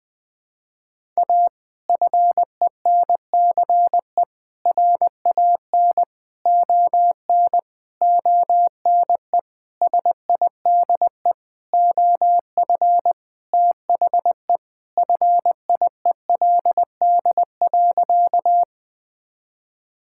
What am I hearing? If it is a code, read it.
Morse code, heard as a keyed tone, about 21 wpm, 700 Hz. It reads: A FENCE RAN ON ONE SIDE OF THE FIELD.
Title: Sentences with 5-Word Phrases - 20wpm